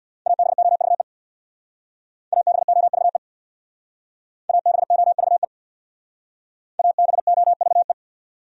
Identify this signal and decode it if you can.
Morse code, heard as a keyed tone, about 64 wpm, 700 Hz. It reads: ABOVE ABOVE ABOVE ABOVE